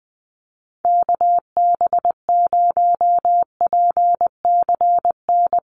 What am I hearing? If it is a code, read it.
Morse code, heard as a keyed tone, about 19 wpm, 700 Hz. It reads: KB0PCN
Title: USA Call Signs Repeated 2x 20wpm